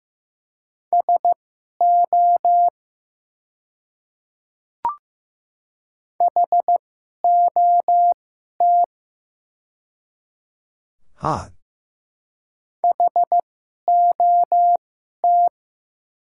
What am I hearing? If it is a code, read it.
Morse code, heard as a keyed tone, about 15 wpm, spaced as for 11 wpm, 700 Hz. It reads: SO HOT HOT
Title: 2x Character Spacing - Top 100 Words - 15wpm